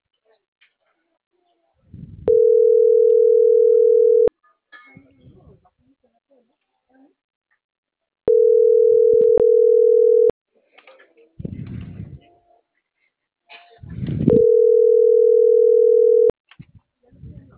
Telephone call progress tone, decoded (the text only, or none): ringback tone